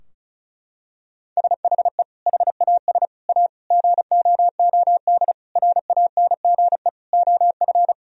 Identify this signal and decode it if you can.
Morse code, heard as a keyed tone, about 36 wpm, 700 Hz. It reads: SHE HAS A GOOD RANGE OF
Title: Sentences from Top 1000 Words - Review Part 4 - 35wpm